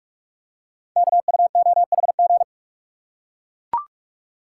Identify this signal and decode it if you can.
Morse code, heard as a keyed tone, about 44 wpm, 690 Hz. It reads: KUOHG
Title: Code Groups - 5 Letters - 45wpm